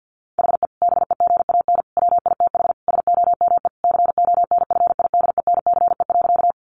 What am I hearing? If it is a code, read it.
Morse code, heard as a keyed tone, about 50 wpm, 700 Hz. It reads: HE BEGAN WITH SOME CONFIDENCE.
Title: Sentences from Top 1500 Words 50wpm